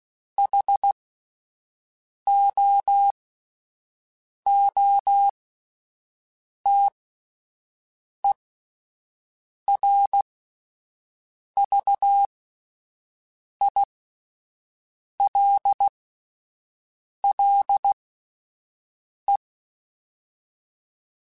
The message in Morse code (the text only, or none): HOOTERVILLE